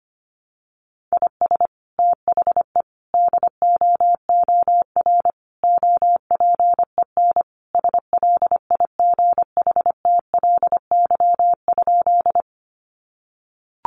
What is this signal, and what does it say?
Morse code, heard as a keyed tone, about 25 wpm, 700 Hz. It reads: IS THE DOOR OPEN SLIGHTLY?
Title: Sentences from Top 1300 Words 25wpm